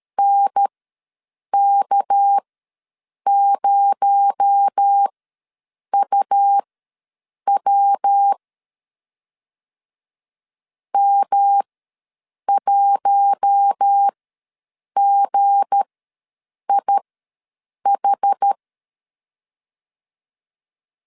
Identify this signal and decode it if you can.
Morse code, heard as a keyed tone, about 13 wpm, spaced as for 7 wpm, 800 Hz. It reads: NK0UW M1GIH
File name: LEZIONI / 7 GF9Z / 7 G ws 12wpm 3140